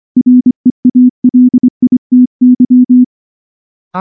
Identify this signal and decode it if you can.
Morse code, heard as a keyed tone, about 23 wpm, 250 Hz. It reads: REALITY
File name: morse / morse-training-25WPM